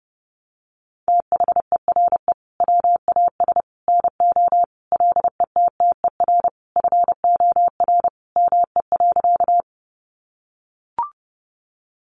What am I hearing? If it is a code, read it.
Morse code, heard as a keyed tone, about 30 wpm, 700 Hz. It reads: THERE WAS NO LETTER FOR ME.
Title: Sentences from Top 300 Words 30wpm